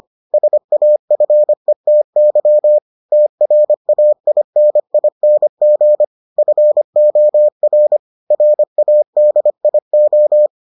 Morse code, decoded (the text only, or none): SAFETY TRAINING FOR RADIO